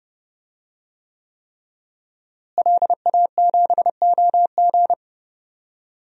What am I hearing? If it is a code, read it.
Morse code, heard as a keyed tone, about 30 wpm, 700 Hz. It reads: LA7OG
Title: DXCC-100 Call Signs - 5 Characters - Repeated 3x - 30wpm